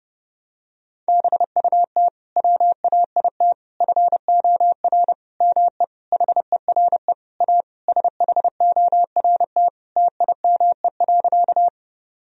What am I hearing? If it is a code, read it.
Morse code, heard as a keyed tone, about 30 wpm, 700 Hz. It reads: BUT WAIT FOR ME HERE A SHORT TIME.
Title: Sentences from Top 1000 Words - Review Part 2 - 30wpm